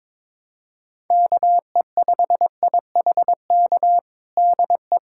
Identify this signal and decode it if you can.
Morse code, heard as a keyed tone, about 21 wpm, 700 Hz. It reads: KE5IHK DE
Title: USA Callsign DE Callsign - 6 Characters - 22wpm